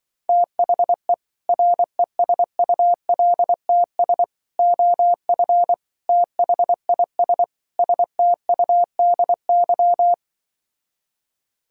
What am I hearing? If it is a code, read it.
Morse code, heard as a keyed tone, about 24 wpm, 700 Hz. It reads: THE RESULTS OF THIS STUDY
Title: Speed-Racing - Sets of 5 Words 20wpm